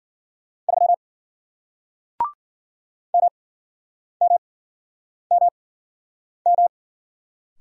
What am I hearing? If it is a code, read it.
Morse code, heard as a keyed tone, about 53 wpm, spaced as for 9 wpm, 700 Hz. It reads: 3 MMMM